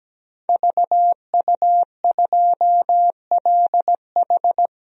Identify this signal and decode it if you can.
Morse code, heard as a keyed tone, about 17 wpm, 700 Hz. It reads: VU2LH
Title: DXCC-100 Call Signs - 5 Characters - 17wpm